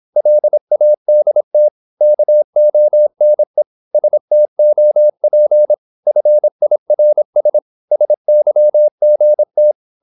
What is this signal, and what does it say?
Morse code, heard as a keyed tone, about 26 wpm, 600 Hz. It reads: LADT KONE STOP FIRS SYGT
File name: Korte ord DK 26wpm